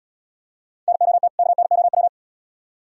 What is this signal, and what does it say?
Morse code, heard as a keyed tone, about 75 wpm, 700 Hz. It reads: NOT KNOW